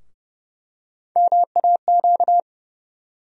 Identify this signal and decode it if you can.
Morse code, heard as a keyed tone, about 30 wpm, 700 Hz. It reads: MAQ
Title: USA Call Sign Postfixes - 3 Letters 30wpm